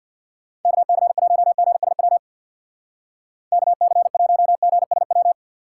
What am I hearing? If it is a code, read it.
Morse code, heard as a keyed tone, about 56 wpm, 700 Hz. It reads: KC1GSW KC1GSW
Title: Speed Racing - USA Call Signs 50wpm